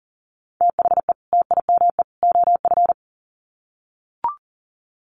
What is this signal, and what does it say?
Morse code, heard as a keyed tone, about 40 wpm, 700 Hz. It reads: THE TIME OF
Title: Sets of 3 Words 40wpm